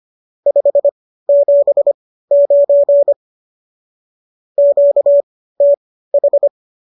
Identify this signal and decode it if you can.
Morse code, heard as a keyed tone, about 25 wpm, spaced as for 15 wpm, 570 Hz. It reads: 579 QTH